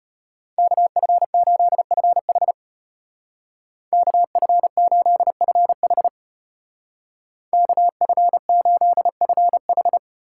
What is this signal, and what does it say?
Morse code, heard as a keyed tone, about 31 wpm, 700 Hz. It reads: KF8FH KF8FH KF8FH